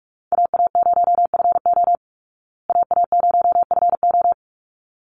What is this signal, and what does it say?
Morse code, heard as a keyed tone, about 44 wpm, 700 Hz. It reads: AA0FO AA0FO
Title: Call Signs Repeated 3x - 45wpm